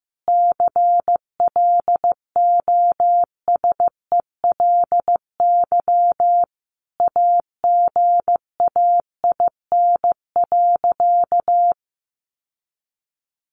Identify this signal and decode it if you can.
Morse code, heard as a keyed tone, about 15 wpm, 700 Hz. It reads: CLOSELY AGAIN.
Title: Sentences from Top 1400 Words 15wpm